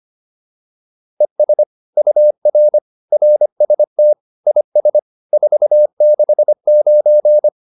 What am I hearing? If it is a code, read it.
Morse code, heard as a keyed tone, about 25 wpm, 600 Hz. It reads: ES UR RST IS 469